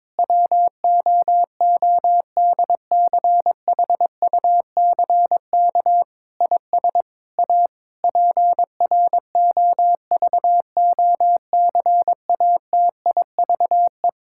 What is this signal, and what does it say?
Morse code, heard as a keyed tone, about 22 wpm, 700 Hz. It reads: WOODCHUCK IS A PROVOCATIVE
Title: Pangram Challenge v2 - 22wpm